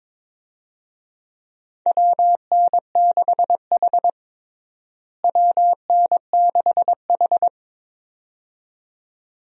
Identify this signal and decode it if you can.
Morse code, heard as a keyed tone, about 22 wpm, 700 Hz. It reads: WN6H WN6H
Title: USA Call Signs - 4 Characters - Repeated 2x - 22wpm